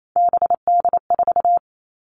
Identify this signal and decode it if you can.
Morse code, heard as a keyed tone, about 28 wpm, 700 Hz. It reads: BD4
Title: Callsign Prefixes for Top 100 DXCC - 28wpm